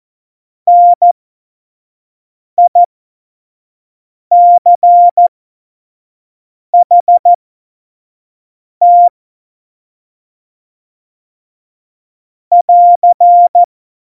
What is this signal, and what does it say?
Morse code, heard as a keyed tone, about 14 wpm, spaced as for 5 wpm, 700 Hz. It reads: NICHT <AR>
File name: Morsekurs / Witze / Witz 26. November 2019, 25 F70